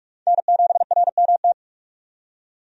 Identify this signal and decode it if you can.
Morse code, heard as a keyed tone, about 41 wpm, 700 Hz. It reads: N7RMT
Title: USA Call Signs - 5 Characters - 45wpm